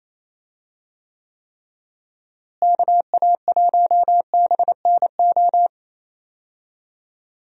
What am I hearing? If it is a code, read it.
Morse code, heard as a keyed tone, about 27 wpm, 700 Hz. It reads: KA1BNO
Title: USA Call Signs - 6 Characters - Repeated 3x - 28wpm